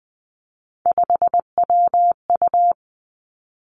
Morse code, heard as a keyed tone, about 20 wpm, 700 Hz. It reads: 5WU
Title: Three Challenging Letter-Number Combinations 20wpm